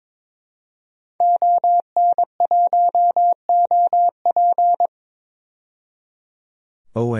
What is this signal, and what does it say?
Morse code, heard as a keyed tone, about 22 wpm, 700 Hz. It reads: ON1OP
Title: DXCC-100 Call Signs - 5 Characters - 22wpm